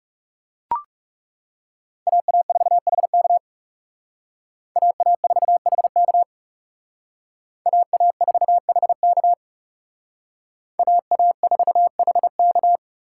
Morse code, heard as a keyed tone, about 35 wpm, 700 Hz. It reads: AA4HK AA4HK AA4HK AA4HK
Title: Speed Racing - USA Call Signs 30wpm